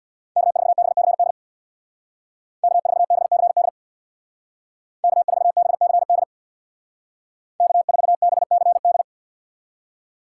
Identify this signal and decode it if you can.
Morse code, heard as a keyed tone, about 63 wpm, 700 Hz. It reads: K4BCD K4BCD K4BCD K4BCD